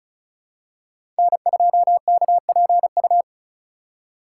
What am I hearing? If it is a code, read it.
Morse code, heard as a keyed tone, about 36 wpm, 700 Hz. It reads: N2KPU
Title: USA Call Signs - 3x Character Spacing - 35wpm